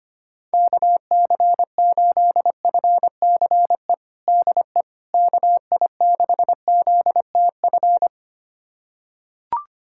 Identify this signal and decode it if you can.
Morse code, heard as a keyed tone, about 25 wpm, 700 Hz. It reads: KC8FCE DE KI6ZTF
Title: USA Callsign DE Callsign - 6 Characters - 25wpm